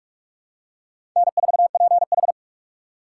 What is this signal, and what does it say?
Morse code, heard as a keyed tone, about 48 wpm, 700 Hz. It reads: N4PH
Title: USA Call Signs - 4 Characters - 2x Character Spacing - 45wpm